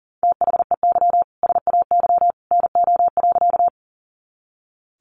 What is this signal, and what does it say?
Morse code, heard as a keyed tone, about 40 wpm, 700 Hz. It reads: THEY SAY NO.
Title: Easy Phrases 40wpm